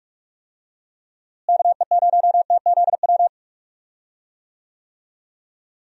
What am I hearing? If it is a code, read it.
Morse code, heard as a keyed tone, about 42 wpm, 700 Hz. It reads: KE0TZW
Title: USA Call Signs - 6 Characters - Repeated 3x - 45wpm